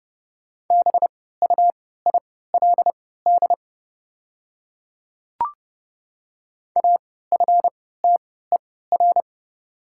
Morse code, heard as a keyed tone, about 30 wpm, spaced as for 17 wpm, 700 Hz. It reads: BUILD AFTER
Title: 3x Character Spacing - Top 200 Words - 30wpm